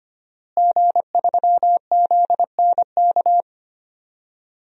Morse code, heard as a keyed tone, about 25 wpm, 700 Hz. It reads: G3ZNK
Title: DXCC-100 Call Signs - 5 Characters - Repeated 3x - 25wpm